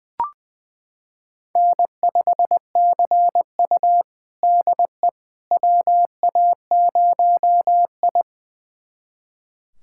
Morse code, heard as a keyed tone, about 20 wpm, 700 Hz. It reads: N5CU DE WA0I
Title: USA Callsign DE Callsign - 4 Characters - 20wpm